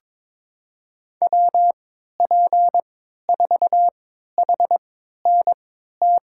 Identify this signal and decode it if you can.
Morse code, heard as a keyed tone, about 22 wpm, spaced as for 12 wpm, 700 Hz. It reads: WP4HNT
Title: USA Call Signs - 3x Character Spacing - 22wpm